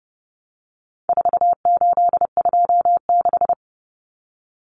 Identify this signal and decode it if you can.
Morse code, heard as a keyed tone, about 30 wpm, 700 Hz. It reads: 4826
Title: Numbers - 4-Digit 30wpm